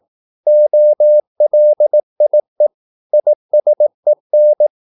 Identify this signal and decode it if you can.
Morse code, heard as a keyed tone, about 18 wpm, 600 Hz. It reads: OLIE ISEN
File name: Korte ord DK 18wpm